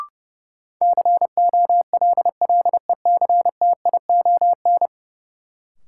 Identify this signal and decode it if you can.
Morse code, heard as a keyed tone, about 30 wpm, 700 Hz. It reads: COLLECTION
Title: Sentences from Top 1200 Words - New Word Drill Alt - 30wpm